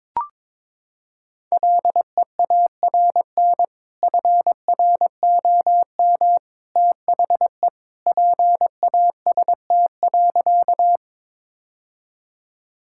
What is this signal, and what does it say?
Morse code, heard as a keyed tone, about 22 wpm, 700 Hz. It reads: LEARN FROM THE PAST.
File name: practice / 22wpm / Sentences with 2-Word Phrases - 22wpm